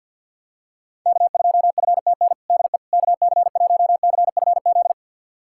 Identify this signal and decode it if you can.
Morse code, heard as a keyed tone, about 50 wpm, 700 Hz. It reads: K2FTN DE KC1CFZ
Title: USA Callsign DE Callsign 50wpm